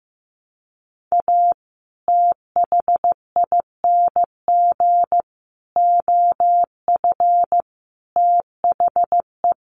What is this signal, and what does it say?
Morse code, heard as a keyed tone, about 15 wpm, 700 Hz. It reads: A THING OF THE
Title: Sets of 5 Words Encore 15wpm